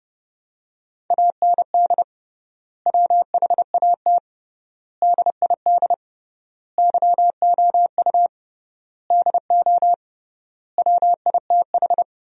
Morse code, heard as a keyed tone, about 30 wpm, 700 Hz. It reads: AND WHAT DID YOU DO WITH
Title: 3x Word-Spacing - Sentences from Top 300 Words - 30wpm